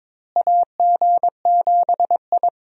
Morse code, heard as a keyed tone, about 22 wpm, 700 Hz. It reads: AG7I
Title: USA Call Signs - 4 Characters - Repeated 3x - 22wpm